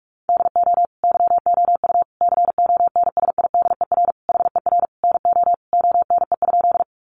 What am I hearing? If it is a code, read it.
Morse code, heard as a keyed tone, about 44 wpm, 700 Hz. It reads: DO YOU CONSIDER HER NO ONE?
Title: Sentences from Top 600 Words 45wpm